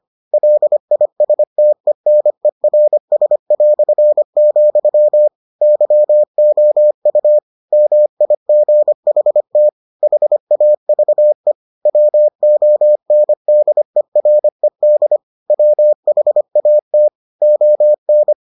Morse code, heard as a keyed tone, about 25 wpm, 600 Hz. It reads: LISTENERS", YOU MIGHT HAVE WONDERED WHAT ON